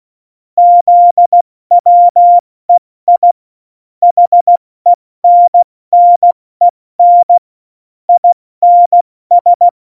Morse code, heard as a keyed tone, about 16 wpm, 700 Hz. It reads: ZWEI HENNEN INS
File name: Morsekurs / Witze / Witz 12 Mai, 2020 70 F80